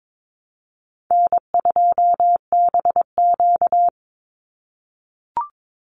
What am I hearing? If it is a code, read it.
Morse code, heard as a keyed tone, about 22 wpm, 700 Hz. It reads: N2BQ
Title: USA Call Signs 22wpm